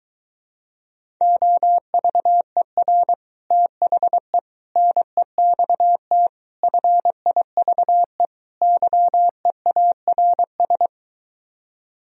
Morse code, heard as a keyed tone, about 23 wpm, 700 Hz. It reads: OVER THE NEXT FIVE YEARS